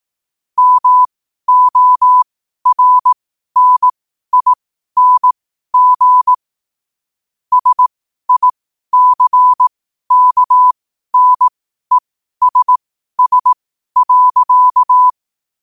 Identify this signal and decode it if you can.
Morse code, heard as a keyed tone, about 18 wpm, spaced as for 12 wpm, 1000 Hz. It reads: MORNING SICKNESS.